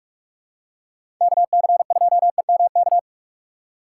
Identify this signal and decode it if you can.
Morse code, heard as a keyed tone, about 46 wpm, 700 Hz. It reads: KC2EMK
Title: USA Call Signs - 6 Characters - 4x Character Spacing - 45wpm